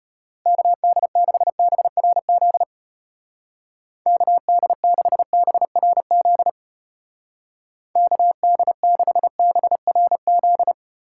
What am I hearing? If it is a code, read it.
Morse code, heard as a keyed tone, about 32 wpm, 700 Hz. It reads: KD6BRZ KD6BRZ KD6BRZ